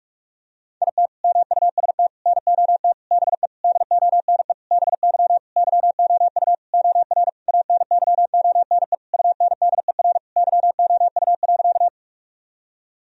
Morse code, heard as a keyed tone, about 46 wpm, 700 Hz. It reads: IT MUST NOT BE DONE BY YOU OR ANYONE UNDER YOU.
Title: Sentences from Top 1000 Words - Review Part 1 - 45wpm